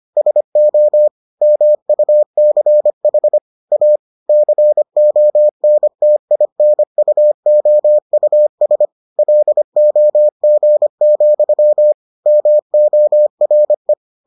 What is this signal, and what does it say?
Morse code, heard as a keyed tone, about 25 wpm, 600 Hz. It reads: SO MUCH A CONTINUOUS LOG, MORE